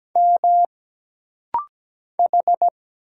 Morse code, heard as a keyed tone, about 17 wpm, 700 Hz. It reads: M H
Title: Single Letter-Number Alt - Rapid-Fire 17wpm